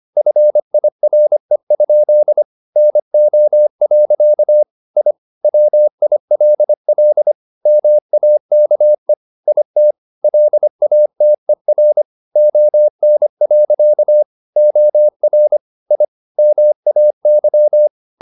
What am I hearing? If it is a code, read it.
Morse code, heard as a keyed tone, about 25 wpm, 600 Hz. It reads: FIRE? NO. I WILL MAKE IT LATER ON. OR I MAY
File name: the old man and the sea part 1 25wpm